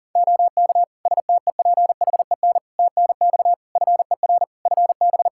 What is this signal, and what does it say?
Morse code, heard as a keyed tone, about 37 wpm, 700 Hz. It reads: OK STEPHEN TNX FER FB
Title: Practice QSO - Exchange 4 - Ending - 40wpm